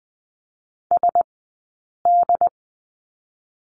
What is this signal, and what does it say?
Morse code, heard as a keyed tone, about 20 wpm, 700 Hz. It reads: S D
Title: 066 Lesson 12d - Letter d - Two Characters 20wpm